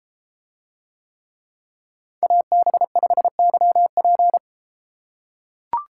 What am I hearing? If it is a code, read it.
Morse code, heard as a keyed tone, about 32 wpm, 700 Hz. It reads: AB5YP